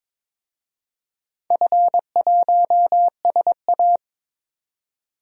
Morse code, heard as a keyed tone, about 22 wpm, 700 Hz. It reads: F1SA
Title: DXCC-100 Call Signs - 4 Characters - 22wpm